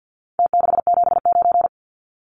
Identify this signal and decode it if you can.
Morse code, heard as a keyed tone, about 50 wpm, 700 Hz. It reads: T679
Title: Numbers - Contest Style - 4-Digit 50wpm